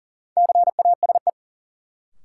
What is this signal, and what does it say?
Morse code, heard as a keyed tone, about 40 wpm, 700 Hz. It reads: CASE
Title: Sets of 2 Words - New Word Drill - 40wpm